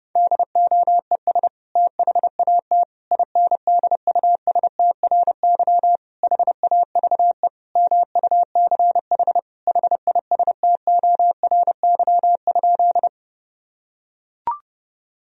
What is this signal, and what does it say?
Morse code, heard as a keyed tone, about 30 wpm, 700 Hz. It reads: DOES THAT INDUSTRY HAVE MUCH HISTORY?
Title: Sentences from Top 1000 Words - Review Part 4 - 30wpm